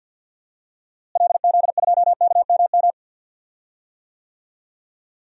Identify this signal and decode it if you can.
Morse code, heard as a keyed tone, about 45 wpm, 700 Hz. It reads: LZ2KMM
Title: DXCC-100 Call Signs - 6 Characters - Repeated 2x - 50wpm